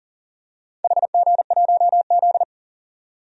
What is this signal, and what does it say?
Morse code, heard as a keyed tone, about 40 wpm, 700 Hz. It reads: HG1Z
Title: DXCC-100 Call Signs - 4 Characters - 40wpm